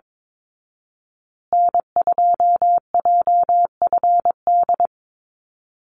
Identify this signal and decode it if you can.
Morse code, heard as a keyed tone, about 22 wpm, 700 Hz. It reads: N2JFD
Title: Call Signs Repeated 2x - 22wpm